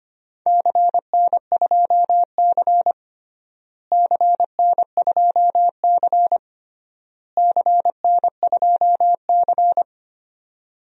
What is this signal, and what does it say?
Morse code, heard as a keyed tone, about 25 wpm, 700 Hz. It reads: CN2C CN2C CN2C
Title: DXCC-100 Call Signs - 4 Characters - Repeated 3x - 25wpm